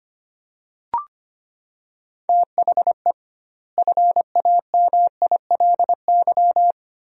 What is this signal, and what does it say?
Morse code, heard as a keyed tone, about 25 wpm, 700 Hz. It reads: THE FAMILY
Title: 2x Word-Spacing - Sentences from Top 400 Words - 25wpm